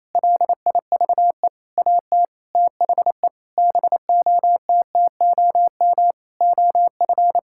Morse code, heard as a keyed tone, about 28 wpm, 700 Hz. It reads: LIVE AT THE BOTTOM OF
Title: Sentences from Top 1000 Words - Review Part 6 - 28wpm